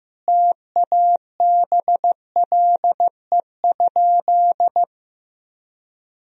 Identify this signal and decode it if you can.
Morse code, heard as a keyed tone, about 15 wpm, 700 Hz. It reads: TABLE?
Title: Sentences from Top 1000 Words - Review Part 4 - 15wpm